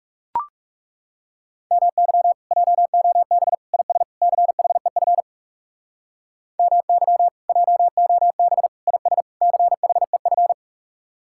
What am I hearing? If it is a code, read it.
Morse code, heard as a keyed tone, about 37 wpm, 700 Hz. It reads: MY JOB IS CHEF MY JOB IS CHEF
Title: Speed-Racing Common QSO Phrases - 30wpm